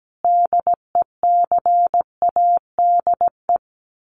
Morse code, heard as a keyed tone, about 17 wpm, 700 Hz. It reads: DECADE
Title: Sets of 5 Words 17wpm